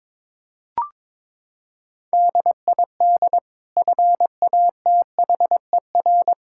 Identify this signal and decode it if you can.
Morse code, heard as a keyed tone, about 22 wpm, 700 Hz. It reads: DID FATHER
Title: Sentences from Top 1600 Words 22wpm